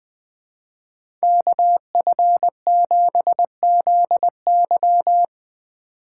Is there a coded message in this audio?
Yes: Morse code from 1.2 to 5.2 s, heard as a keyed tone, about 20 wpm, 700 Hz. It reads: KF7ZY